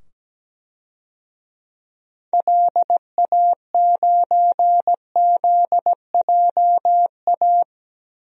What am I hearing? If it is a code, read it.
Morse code, heard as a keyed tone, about 17 wpm, 700 Hz. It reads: LA9ZJA
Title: DXCC-100 Call Signs - 6 Characters - 17wpm